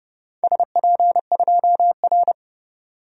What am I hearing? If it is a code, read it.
Morse code, heard as a keyed tone, about 28 wpm, 700 Hz. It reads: SP2R
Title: DXCC-100 Call Signs - 4 Characters - 30wpm